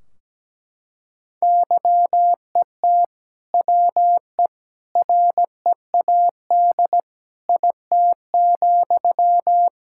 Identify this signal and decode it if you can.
Morse code, heard as a keyed tone, about 17 wpm, 700 Hz. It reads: YET WE READ IT,
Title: Sentences from Top 1600 Words 17wpm